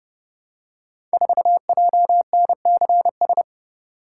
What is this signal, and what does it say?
Morse code, heard as a keyed tone, about 30 wpm, 700 Hz. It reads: 4JNCS
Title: Code Groups - 5 Letter:Numbers - 30wpm